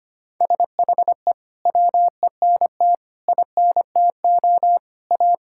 Morse code, heard as a keyed tone, about 25 wpm, 700 Hz. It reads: SHE WENT INTO A
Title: Sentences from Top 1000 Words - Review Part 6 - 25wpm